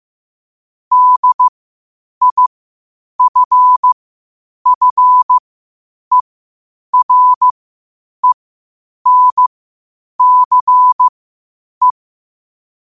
Morse code, heard as a keyed tone, about 15 wpm, spaced as for 8 wpm, 1000 Hz. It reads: DIFFERENCE
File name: qotdfiles / 8wpm